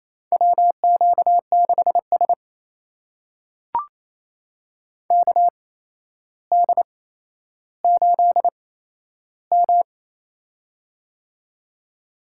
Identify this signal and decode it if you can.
Morse code, heard as a keyed tone, about 28 wpm, spaced as for 12 wpm, 700 Hz. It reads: WQ6S KD8M
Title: USA Call Signs - 4 Characters - 8x Character Spacing - 28wpm